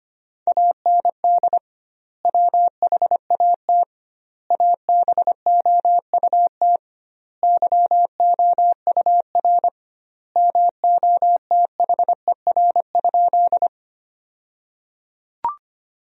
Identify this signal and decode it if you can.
Morse code, heard as a keyed tone, about 25 wpm, 700 Hz. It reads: AND WHAT ABOUT YOUR MOTHER?
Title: 2x Word-Spacing - Sentences from Top 200 Words - 25wpm